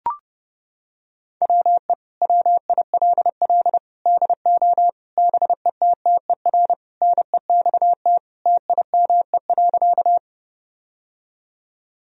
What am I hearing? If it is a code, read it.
Morse code, heard as a keyed tone, about 30 wpm, 700 Hz. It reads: WE WILL DO BETTER NEXT TIME.
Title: Sentences from Top 1000 Words - Review Part 1 - 30wpm